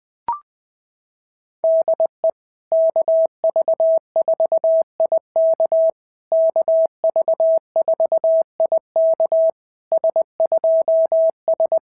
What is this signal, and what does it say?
Morse code, heard as a keyed tone, about 20 wpm, 650 Hz. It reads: DE KV4IK KV4IK S2S
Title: SOTA QSO Practice 20wpm